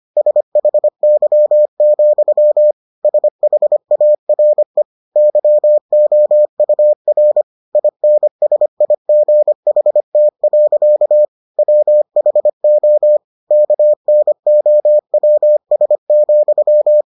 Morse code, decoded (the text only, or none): SHY, SHARE YOUR INSIGHT. WHO KNOWS,